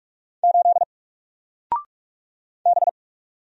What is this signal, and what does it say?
Morse code, heard as a keyed tone, about 44 wpm, 700 Hz. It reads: 8 B
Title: Single Letter-Number Alt 45wpm